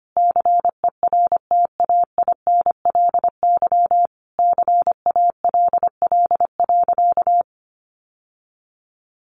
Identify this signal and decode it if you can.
Morse code, heard as a keyed tone, about 25 wpm, 700 Hz. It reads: CERTAINLY CALL.
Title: Sentences from Top 900 Words 25wpm